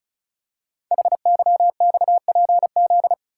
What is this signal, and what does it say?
Morse code, heard as a keyed tone, about 36 wpm, 700 Hz. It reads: HYXPZ
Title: Code Groups - 5 Letters - 35wpm